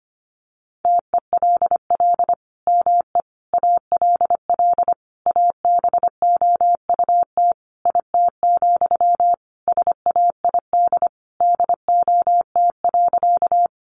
Morse code, heard as a keyed tone, about 25 wpm, 700 Hz. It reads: TELL ME ALL ABOUT IT, SAID DOT.
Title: Sentences from Top 1500 Words 25wpm